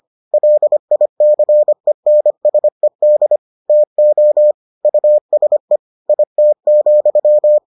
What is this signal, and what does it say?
Morse code, heard as a keyed tone, about 25 wpm, 600 Hz. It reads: LICENSED TO USE IT,